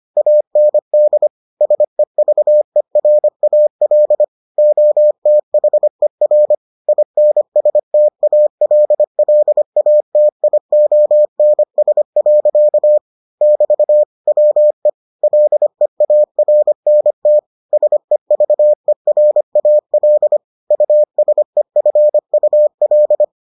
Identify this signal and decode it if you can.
Morse code, heard as a keyed tone, about 25 wpm, 600 Hz. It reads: AND SEVERAL OTHER INSTALLATIONS. <BT> WE LEARNT SEVERAL USEFUL